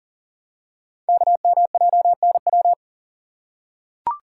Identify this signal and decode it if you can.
Morse code, heard as a keyed tone, about 40 wpm, 700 Hz. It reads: KMJNW
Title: Code Groups - 5 Letters - 40wpm